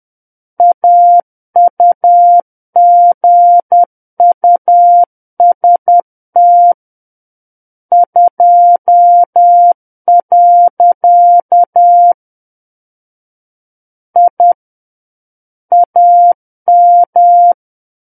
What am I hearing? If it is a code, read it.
Morse code, heard as a keyed tone, about 10 wpm, 700 Hz. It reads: AUGUST 2. I AM
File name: Book One / 01-Chapter 1 - 10 WPM